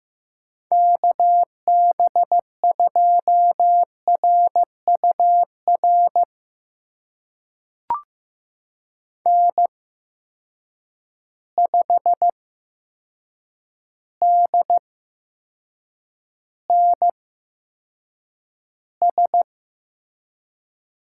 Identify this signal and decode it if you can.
Morse code, heard as a keyed tone, about 15 wpm, spaced as for 7 wpm, 700 Hz. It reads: KB2RUR N5DNS
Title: USA Call Signs - 8x Character Spacing - 15wpm